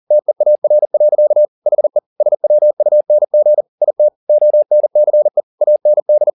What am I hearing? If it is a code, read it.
Morse code, heard as a keyed tone, about 40 wpm, 600 Hz. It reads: TEAR. HE SWUNG IT ONCE AND
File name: the old man and the sea 2 40wpm